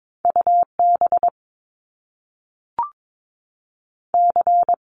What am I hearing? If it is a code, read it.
Morse code, heard as a keyed tone, about 22 wpm, 700 Hz. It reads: UB C
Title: Top Two Letter Combinations in Words 22wpm